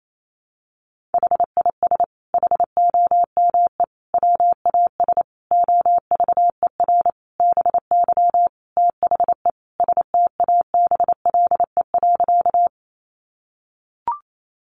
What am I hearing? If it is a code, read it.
Morse code, heard as a keyed tone, about 28 wpm, 700 Hz. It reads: HIS HOME WAS OVER BY THE STABLE.
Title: Sentences from Top 1500 Words 28wpm